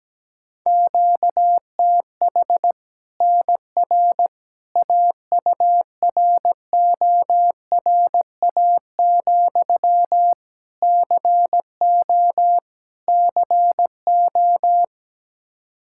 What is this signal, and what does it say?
Morse code, heard as a keyed tone, about 17 wpm, 700 Hz. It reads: QTH NR AURORA, CO CO
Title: Practice QSO - Exchange 1 - Beginning - 17wpm